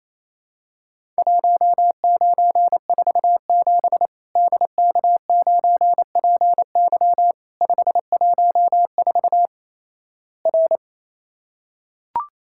Morse code, heard as a keyed tone, about 28 wpm, 700 Hz. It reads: 1947 DK9PY 514 R
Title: Worked All Europe DX Contest - QTC Exchange - 28wpm